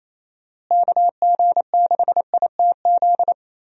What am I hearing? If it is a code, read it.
Morse code, heard as a keyed tone, about 27 wpm, 700 Hz. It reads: KG6ITZ
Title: USA Call Signs - 6 Characters - Repeated 2x - 28wpm